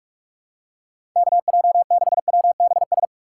Morse code, heard as a keyed tone, about 46 wpm, 700 Hz. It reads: KJ6WBS
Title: USA Call Signs - 6 Characters - 2x Character Spacing - 45wpm